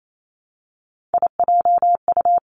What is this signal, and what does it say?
Morse code, heard as a keyed tone, about 28 wpm, 700 Hz. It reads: IJU
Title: Three Challenging Letter-Number Combinations 28wpm